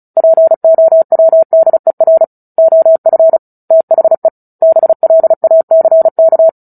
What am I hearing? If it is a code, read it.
Morse code, heard as a keyed tone, about 34 wpm, 650 Hz. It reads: POWDER OF THE BLACK
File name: Book Two / 09-Chapter 9 - 35 WPM